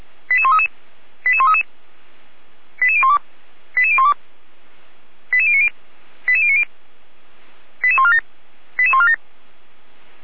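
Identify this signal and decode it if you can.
Five-tone selective call (ZVEI), heard as quick runs of five tones, tones of about 70 ms. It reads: 88133 88133 88012 88012 88090 88090 88147 88147
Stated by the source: ZVEI five-tone call received by SDR.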